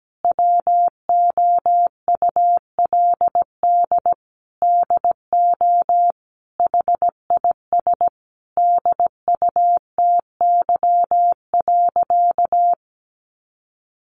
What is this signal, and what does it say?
Morse code, heard as a keyed tone, about 17 wpm, 700 Hz. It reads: WOULD DO HIS DUTY.
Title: Sentences from Top 1500 Words 17wpm